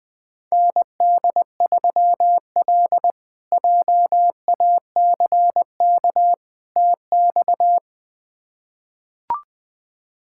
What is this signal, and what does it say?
Morse code, heard as a keyed tone, about 20 wpm, 700 Hz. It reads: ND3L JACK TX
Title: K1USN SST QSO Practice - 20wpm